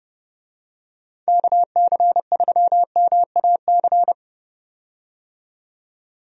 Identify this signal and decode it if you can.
Morse code, heard as a keyed tone, about 30 wpm, 700 Hz. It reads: KC3MAC
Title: USA Call Signs - 6 Characters - 30wpm